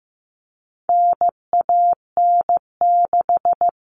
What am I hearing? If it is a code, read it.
Morse code, heard as a keyed tone, about 15 wpm, 700 Hz. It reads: NAN6
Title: All Cut Numbers - 4-Digit 15wpm